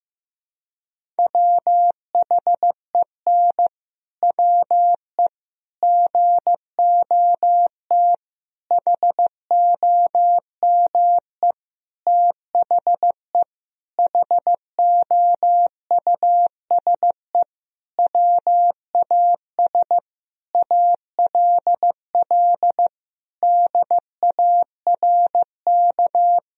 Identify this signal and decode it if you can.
Morse code, heard as a keyed tone, about 16 wpm, 700 Hz. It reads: WHEN WE GOT HOME THE HOUSE WAS ALL DARK
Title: Sentences from Top 500 Words and Abbreviations 15wpm